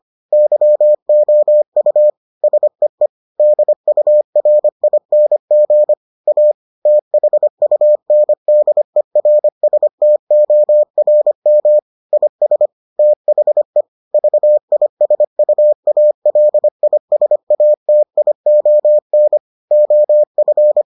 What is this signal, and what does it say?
Morse code, heard as a keyed tone, about 25 wpm, 600 Hz. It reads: YOU SEE DURING A THUNDERSTORM IS THE VISUALISATION OF